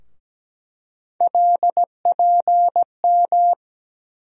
Morse code, heard as a keyed tone, about 17 wpm, 700 Hz. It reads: LPM
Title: USA Call Sign Postfixes - 3 Letters 17wpm